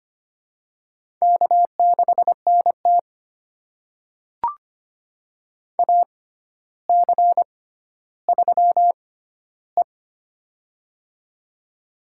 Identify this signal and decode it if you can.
Morse code, heard as a keyed tone, about 25 wpm, spaced as for 13 wpm, 700 Hz. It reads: K6NT AC3E
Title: USA Call Signs - 4 Characters - 6x Character Spacing - 25wpm